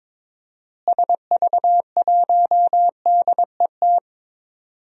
Morse code, heard as a keyed tone, about 22 wpm, 700 Hz. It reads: SV1DET
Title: DXCC-100 Call Signs - 6 Characters - Repeated 3x - 22wpm